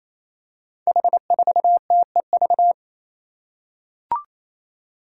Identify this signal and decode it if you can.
Morse code, heard as a keyed tone, about 29 wpm, 700 Hz. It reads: H4TEV
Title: Five Challenging Letter-Number Combinations 28wpm